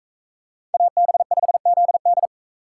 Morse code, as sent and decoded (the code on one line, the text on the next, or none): .- -... ..... --.. -..
AB5ZD